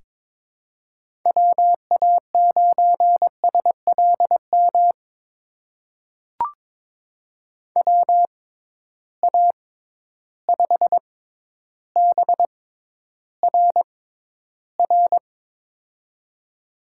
Morse code, heard as a keyed tone, about 22 wpm, spaced as for 11 wpm, 700 Hz. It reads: WA9SLM WA5BRR
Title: USA Call Signs - 6 Characters - 6x Character Spacing - 22wpm